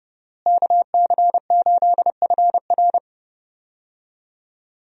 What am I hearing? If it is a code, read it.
Morse code, heard as a keyed tone, about 29 wpm, 700 Hz. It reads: KC8FR
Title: USA Call Signs - 5 Characters - Repeated 2x - 30wpm